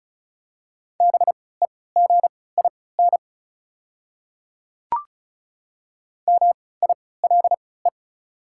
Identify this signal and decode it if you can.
Morse code, heard as a keyed tone, about 34 wpm, spaced as for 20 wpm, 690 Hz. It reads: BEGIN MILE
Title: 3x Character Spacing - Top 500 Words - 35wpm